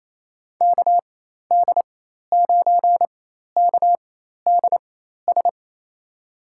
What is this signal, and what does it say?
Morse code, heard as a keyed tone, about 28 wpm, spaced as for 13 wpm, 700 Hz. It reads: KD9KDS
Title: USA Call Signs - 4x Character Spacing - 28wpm